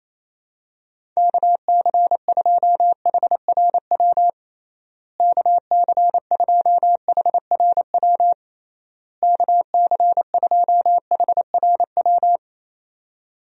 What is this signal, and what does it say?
Morse code, heard as a keyed tone, about 28 wpm, 700 Hz. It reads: KC2HRW KC2HRW KC2HRW
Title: USA Call Signs - 6 Characters - Repeated 3x - 28wpm